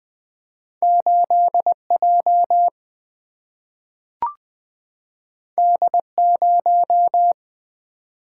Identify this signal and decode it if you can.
Morse code, heard as a keyed tone, about 20 wpm, 700 Hz. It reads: 8 J D 0
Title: Single-Letter Single-Number 20wpm